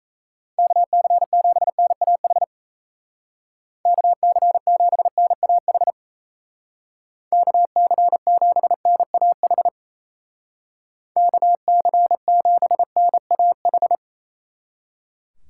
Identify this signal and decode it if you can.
Morse code, heard as a keyed tone, about 33 wpm, 700 Hz. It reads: KC7NAH KC7NAH KC7NAH KC7NAH
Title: Speed Racing - USA 2x3 Call Signs 28wpm